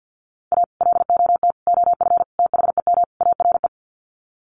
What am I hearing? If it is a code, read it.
Morse code, heard as a keyed tone, about 50 wpm, 700 Hz. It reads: A LOT OF THEM ARE
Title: Sets of 5 Words Encore2 50wpm